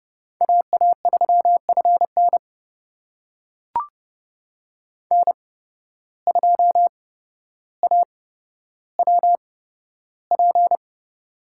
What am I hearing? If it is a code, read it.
Morse code, heard as a keyed tone, about 32 wpm, spaced as for 13 wpm, 700 Hz. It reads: AA3FN N2AWP
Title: USA Call Signs - 5 Characters - 8x Character Spacing - 30wpm